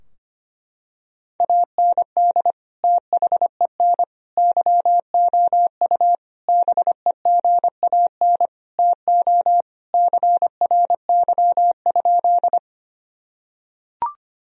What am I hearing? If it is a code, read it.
Morse code, heard as a keyed tone, about 25 wpm, 700 Hz. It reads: AND THEN YOU BEGAN TO CRY?
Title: Sentences from Top 1000 Words - Review Part 2 - 25wpm